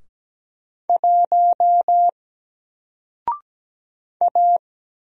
Morse code, heard as a keyed tone, about 17 wpm, 700 Hz. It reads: 1 A
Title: Single Character (Letter:Number:Punctuation:Prosign) 17wpm